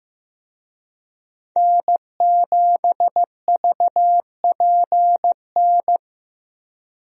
Morse code, heard as a keyed tone, about 15 wpm, 700 Hz. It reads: N7VPN
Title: USA Call Signs - 5 Characters - 15wpm